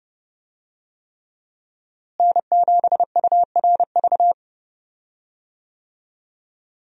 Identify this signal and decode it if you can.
Morse code, heard as a keyed tone, about 29 wpm, 700 Hz. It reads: N7URV
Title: USA Call Signs - 5 Characters - Repeated 3x - 30wpm